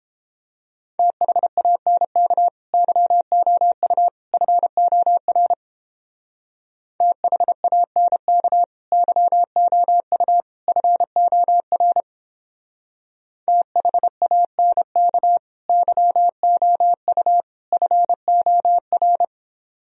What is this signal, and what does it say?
Morse code, heard as a keyed tone, about 29 wpm, 700 Hz. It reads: THANK YOU FOR THANK YOU FOR THANK YOU FOR